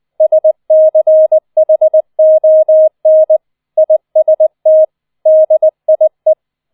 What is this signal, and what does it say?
Morse code, heard as a keyed tone, about 20 wpm, 630 Hz. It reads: SCHON IST DIE